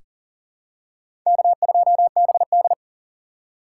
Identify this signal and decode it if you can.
Morse code, heard as a keyed tone, about 40 wpm, 700 Hz. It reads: K2BD
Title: USA Call Signs - 4 Characters - 4x Character Spacing - 40wpm